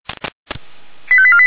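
Five-tone selective call (ZVEI), heard as quick runs of five tones, tones of about 70 ms. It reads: 85848
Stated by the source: ZVEI five-tone call received by SDR, from feuerwehr-orlamuende.org fire brigade archive.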